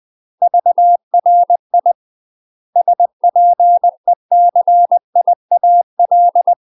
Morse code, heard as a keyed tone, about 20 wpm, 700 Hz. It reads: VRI SPECIAL
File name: UK MASTER QSOs TRAINING / 20WPM / UK MASTER QSOs 0036